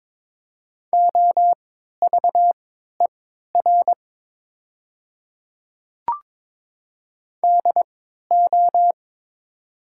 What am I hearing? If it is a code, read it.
Morse code, heard as a keyed tone, about 23 wpm, spaced as for 13 wpm, 700 Hz. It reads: OVER DO
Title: 3x Character Spacing - Top 100 Words - 22wpm